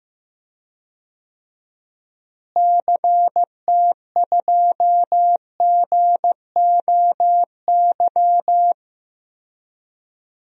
Morse code, heard as a keyed tone, about 15 wpm, 700 Hz. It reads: CT2GOY
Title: DXCC-100 Call Signs - 6 Characters - Repeated 2x - 15wpm